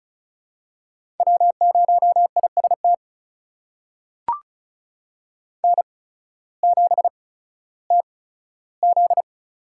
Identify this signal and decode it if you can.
Morse code, heard as a keyed tone, about 36 wpm, spaced as for 16 wpm, 700 Hz. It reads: W0IST N7TZ
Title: USA Call Signs - 5 Characters - 8x Character Spacing - 35wpm